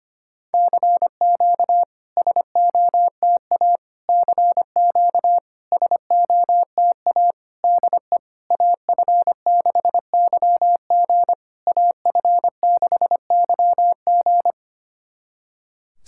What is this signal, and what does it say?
Morse code, heard as a keyed tone, about 25 wpm, 690 Hz. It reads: CQ SOTA CQ SOTA DE AF6YG AF6YG
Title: SOTA QSO Practice 25wpm